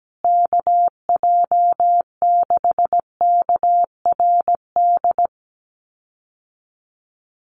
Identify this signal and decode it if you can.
Morse code, heard as a keyed tone, about 17 wpm, 700 Hz. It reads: KJ6KRD
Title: USA Call Signs Repeated 2x 17wpm